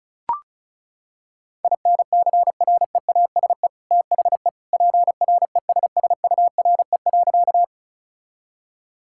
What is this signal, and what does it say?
Morse code, heard as a keyed tone, about 34 wpm, 700 Hz. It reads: INCREASE THE PRESSURE.
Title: Sentences from Top 1000 Words - Review Part 6 - 35wpm